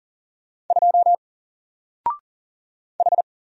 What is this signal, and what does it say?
Morse code, heard as a keyed tone, about 40 wpm, 700 Hz. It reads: 2 H
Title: Single Character (Letter:Number:Punctuation:Prosign) Alt 40wpm